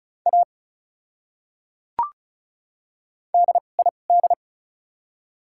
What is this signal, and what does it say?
Morse code, heard as a keyed tone, about 36 wpm, 700 Hz. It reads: A D I D
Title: 2x Character Spacing - Top 100 Words - 35wpm